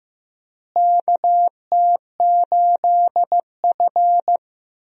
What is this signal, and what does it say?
Morse code, heard as a keyed tone, about 15 wpm, 700 Hz. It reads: KT8F
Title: USA Call Signs - 4 Characters - Repeated 3x - 15wpm